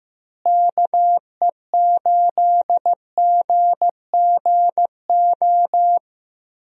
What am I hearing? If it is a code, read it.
Morse code, heard as a keyed tone, about 15 wpm, 700 Hz. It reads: KE8GGO
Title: USA Call Signs - 6 Characters - Repeated 2x - 15wpm